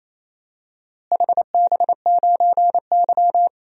Morse code, heard as a keyed tone, about 28 wpm, 700 Hz. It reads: HB9Y
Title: DXCC-100 Call Signs - 4 Characters - Repeated 3x - 28wpm